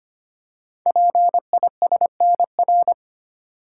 Morse code, heard as a keyed tone, about 25 wpm, 700 Hz. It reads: PISNR
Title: Code Groups - 5 Letters - 25wpm